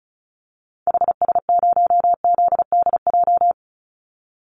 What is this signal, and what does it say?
Morse code, heard as a keyed tone, about 34 wpm, 700 Hz. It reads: HS0ZDJ
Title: Call Signs - 35wpm